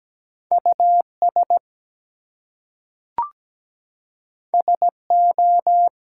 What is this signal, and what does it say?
Morse code, heard as a keyed tone, about 17 wpm, 700 Hz. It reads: US SO
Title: Two Letter Words - 17wpm